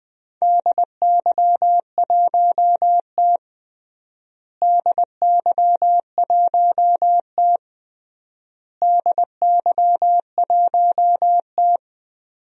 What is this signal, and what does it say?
Morse code, heard as a keyed tone, about 21 wpm, 700 Hz. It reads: DY1T DY1T DY1T
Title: DXCC-100 Call Signs - 4 Characters - Repeated 3x - 20wpm